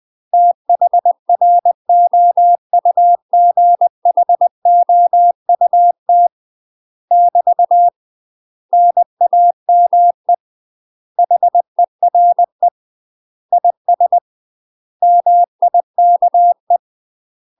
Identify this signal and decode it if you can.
Morse code, heard as a keyed tone, about 20 wpm, 700 Hz. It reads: THROUGHOUT <BT> NAME HERE IS MIKE
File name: UK MASTER QSOs TRAINING / 20WPM / UK MASTER QSOs 0034